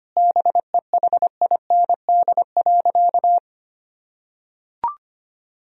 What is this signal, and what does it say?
Morse code, heard as a keyed tone, about 25 wpm, 700 Hz. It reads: BEHIND.
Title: Sentences from Top 1000 Words - Review Part 1 - 25wpm